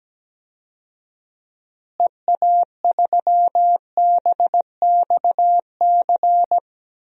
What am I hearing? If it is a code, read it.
Morse code, heard as a keyed tone, about 17 wpm, 700 Hz. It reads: EA3BXC
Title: DXCC-100 Call Signs - 6 Characters - Repeated 3x - 17wpm